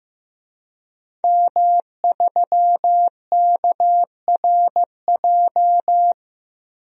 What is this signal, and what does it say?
Morse code, heard as a keyed tone, about 15 wpm, 700 Hz. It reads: M3KRJ
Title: DXCC-100 Call Signs - 5 Characters - 15wpm